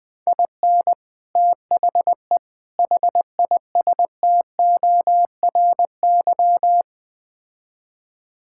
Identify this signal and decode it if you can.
Morse code, heard as a keyed tone, about 20 wpm, 700 Hz. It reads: IN THE HISTORY
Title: Build A Phrase - 5-Word Phrases - 20wpm